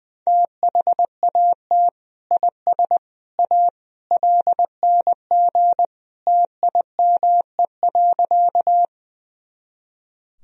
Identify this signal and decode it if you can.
Morse code, heard as a keyed tone, about 21 wpm, 700 Hz. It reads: THAT IS A LNG TIME.
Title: Sentences from Top 100 Words and Abbreviations 20wpm